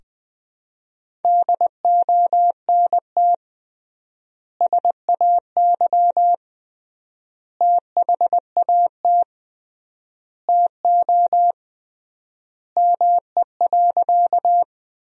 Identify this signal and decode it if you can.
Morse code, heard as a keyed tone, about 20 wpm, 700 Hz. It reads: DONT SAY THAT TO ME.
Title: 3x Word-Spacing - Sentences from Top 300 Words - 20wpm